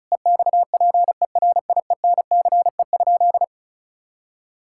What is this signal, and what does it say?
Morse code, heard as a keyed tone, about 36 wpm, 700 Hz. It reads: EXPERIENCE?
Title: Sentences from Top 1600 Words 35wpm